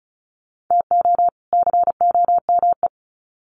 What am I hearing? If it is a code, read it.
Morse code, heard as a keyed tone, about 34 wpm, 700 Hz. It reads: TO COME
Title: Sets of 2 Words 35wpm